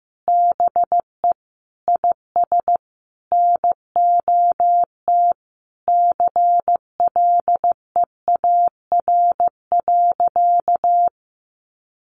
Morse code, heard as a keyed tone, about 15 wpm, 700 Hz. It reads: BE IS NOT CLEAR.
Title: Sentences from Top 1300 Words 15wpm